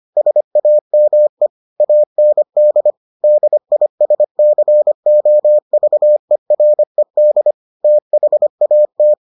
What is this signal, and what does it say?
Morse code, heard as a keyed tone, about 25 wpm, 600 Hz. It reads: SAME AND DISCOVERED THAT